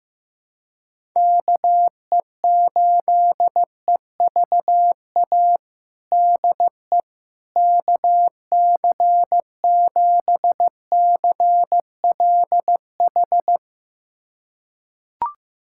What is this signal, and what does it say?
Morse code, heard as a keyed tone, about 15 wpm, 700 Hz. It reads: KE8EVA DE KC7CLH
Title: USA Callsign DE Callsign 15wpm